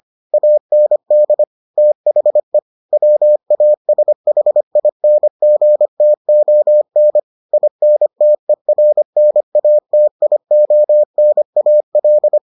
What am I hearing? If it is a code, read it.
Morse code, heard as a keyed tone, about 25 wpm, 600 Hz. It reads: AND THE WASHINGTON INTERNATIONAL